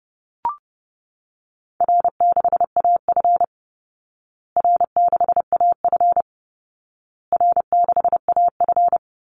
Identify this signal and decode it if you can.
Morse code, heard as a keyed tone, about 30 wpm, 700 Hz. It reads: R6AF R6AF R6AF
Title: Easy Call Signs Repeated 3x - 30wpm